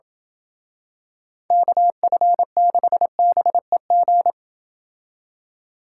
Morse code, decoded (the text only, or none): KF6BEG